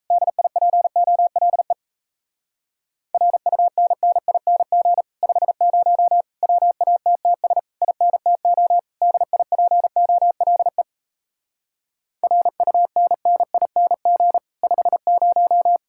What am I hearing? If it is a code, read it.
Morse code, heard as a keyed tone, about 38 wpm, 700 Hz. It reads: DIPOLE RUNNING 50 WATTS INTO DIPOLE RUNNING 50